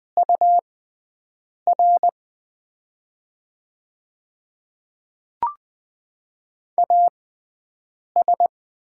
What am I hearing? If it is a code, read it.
Morse code, heard as a keyed tone, about 20 wpm, spaced as for 7 wpm, 700 Hz. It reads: UR AS
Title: 6x Character Spacing - Top 500 Words - 20wpm